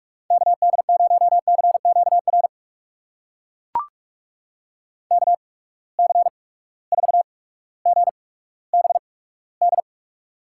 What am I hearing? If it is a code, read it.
Morse code, heard as a keyed tone, about 42 wpm, spaced as for 19 wpm, 700 Hz. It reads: KD0CQR KC4GBD
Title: USA Call Signs - 6 Characters - 8x Character Spacing - 45wpm